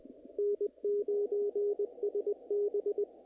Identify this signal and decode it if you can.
Morse code, heard as a keyed tone, about 23 wpm, 410 Hz, with radio noise. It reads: N9SB
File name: contests / naqp-cw-k5tr-op-k5pi- -1900